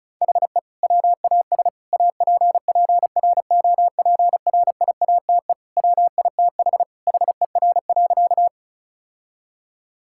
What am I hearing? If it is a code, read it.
Morse code, heard as a keyed tone, about 34 wpm, 700 Hz. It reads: HE WAS APPROPRIATE WITH HER.
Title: Sentences from Top 1000 Words - Review Part 5 - 35wpm